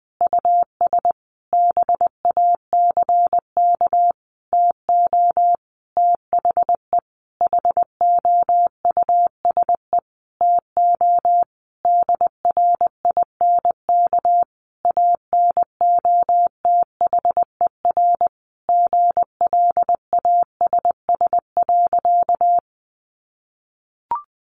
Morse code, heard as a keyed tone, about 19 wpm, 700 Hz. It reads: US BACK TO THE HOUSE TO DRINK ANOTHER GLASS.
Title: Sentences from Top 1500 Words 20wpm